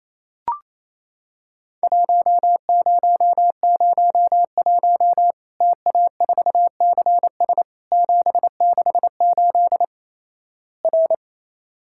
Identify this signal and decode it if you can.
Morse code, heard as a keyed tone, about 28 wpm, 700 Hz. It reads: 1001 TA4CS 768 R
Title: Worked All Europe DX Contest - QTC Exchange - 28wpm